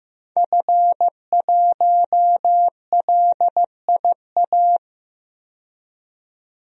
Morse code, heard as a keyed tone, about 15 wpm, 700 Hz. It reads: F1LIA
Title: DXCC-100 Call Signs - 5 Characters - 15wpm